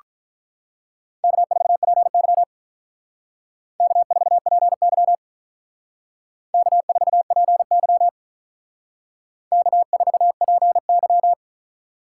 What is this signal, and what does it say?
Morse code, heard as a keyed tone, about 42 wpm, 700 Hz. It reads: K4PY K4PY K4PY K4PY
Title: Speed Racing - USA 1x2 Call Signs 35wpm